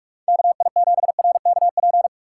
Morse code, heard as a keyed tone, about 56 wpm, 700 Hz. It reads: KI7RKP
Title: Speed Racing - USA 2x3 Call Signs 45wpm